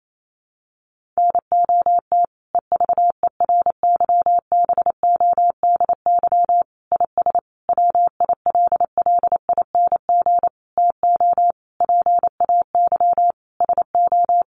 Morse code, heard as a keyed tone, about 28 wpm, 700 Hz. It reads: NOT EVERYBODY IS WILLING TO PAY SO
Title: Sentences from Top 1400 Words 28wpm